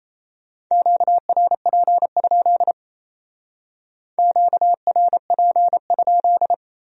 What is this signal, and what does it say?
Morse code, heard as a keyed tone, about 36 wpm, 700 Hz. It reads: QRP? QRP?